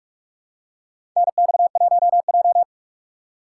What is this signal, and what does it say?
Morse code, heard as a keyed tone, about 45 wpm, 700 Hz. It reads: NX1J
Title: USA Call Signs - 3x Character Spacing - 45wpm